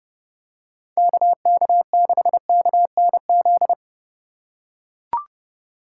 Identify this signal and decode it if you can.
Morse code, heard as a keyed tone, about 31 wpm, 700 Hz. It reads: KK6KNZ
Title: USA Call Signs - 6 Characters - 2x Character Spacing - 30wpm